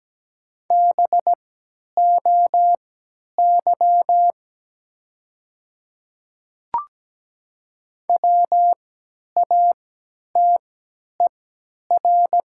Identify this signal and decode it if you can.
Morse code, heard as a keyed tone, about 17 wpm, spaced as for 10 wpm, 700 Hz. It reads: BOY WATER
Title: 3x Character Spacing - Top 200 Words - 17wpm